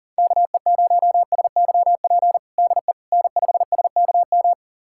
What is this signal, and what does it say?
Morse code, heard as a keyed tone, about 37 wpm, 700 Hz. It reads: KE0SYP DE N5SKM
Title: USA Callsign DE Callsign 40wpm